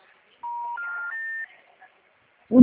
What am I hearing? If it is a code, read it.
Special information tone.